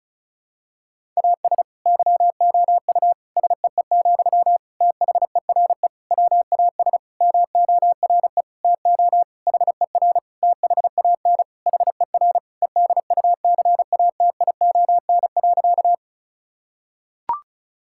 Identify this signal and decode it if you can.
Morse code, heard as a keyed tone, about 34 wpm, 700 Hz. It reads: AS YOU SEE, THERE WAS MORE TO HER THAN HER EDUCATION.
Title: Sentences from Top 1000 Words - Review Part 6 - 35wpm